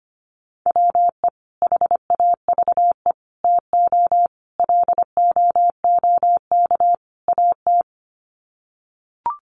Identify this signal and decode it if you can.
Morse code, heard as a keyed tone, about 25 wpm, 700 Hz. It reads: WE HAVE TO LOOK AT
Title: Sets of 5 Words Encore2 25wpm